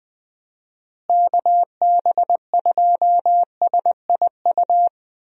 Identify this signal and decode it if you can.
Morse code, heard as a keyed tone, about 20 wpm, 700 Hz. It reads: KB2SIU
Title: USA Call Signs - 6 Characters - Repeated 2x - 20wpm